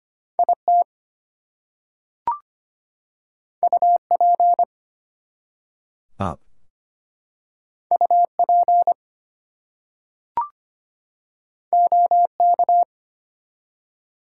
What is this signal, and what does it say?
Morse code, heard as a keyed tone, about 25 wpm, 700 Hz. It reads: IT UP UP OK
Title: Two Letter Words - 25wpm